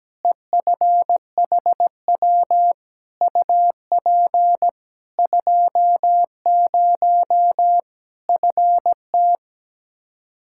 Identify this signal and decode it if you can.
Morse code, heard as a keyed tone, about 17 wpm, 700 Hz. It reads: EFHW UP 20 FT
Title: Practice QSO - Exchange 2 - Info - 17wpm